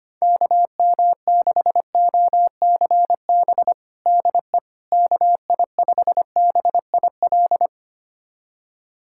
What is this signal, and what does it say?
Morse code, heard as a keyed tone, about 25 wpm, 700 Hz. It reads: KM6OCB DE KI5BIL
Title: USA Callsign DE Callsign - 6 Characters - 25wpm